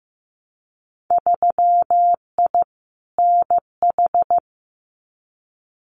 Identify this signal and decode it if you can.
Morse code, heard as a keyed tone, about 15 wpm, 700 Hz. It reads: 3I NH
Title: Winter Field Day Exchanges Encore 15wpm